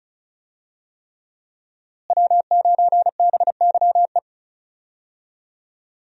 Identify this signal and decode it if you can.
Morse code, heard as a keyed tone, about 34 wpm, 700 Hz. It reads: W9BYE
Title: USA Call Signs - 5 Characters - Repeated 3x - 35wpm